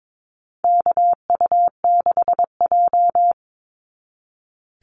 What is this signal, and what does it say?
Morse code, heard as a keyed tone, about 22 wpm, 700 Hz. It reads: KU6J
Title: NA SOTA Activator Callsigns 22wpm